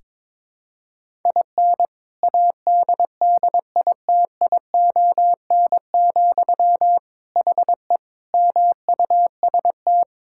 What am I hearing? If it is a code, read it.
Morse code, heard as a keyed tone, about 22 wpm, 700 Hz. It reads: IN ADDITION, HE MUST
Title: Sentences from Top 1000 Words - Review Part 6 - 22wpm